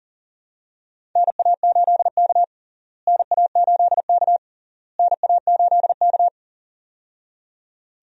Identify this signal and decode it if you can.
Morse code, heard as a keyed tone, about 40 wpm, 700 Hz. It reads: NA8K NA8K NA8K
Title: USA Call Signs - 4 Characters - Repeated 3x - 40wpm